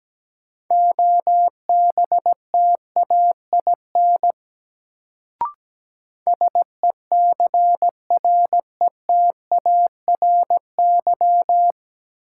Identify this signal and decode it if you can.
Morse code, heard as a keyed tone, about 17 wpm, 700 Hz. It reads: OBTAIN SECRETARY
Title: Sentences from Top 1300 Words - New Word Drill Alt - 17wpm